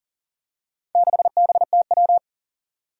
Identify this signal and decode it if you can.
Morse code, heard as a keyed tone, about 37 wpm, 700 Hz. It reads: 6BTW
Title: Four Challenging Letter-Number Combinations 40wpm